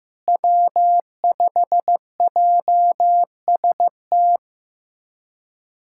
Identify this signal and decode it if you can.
Morse code, heard as a keyed tone, about 15 wpm, 700 Hz. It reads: W5JST
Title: USA Callsign DE Callsign - 5 Characters - 15wpm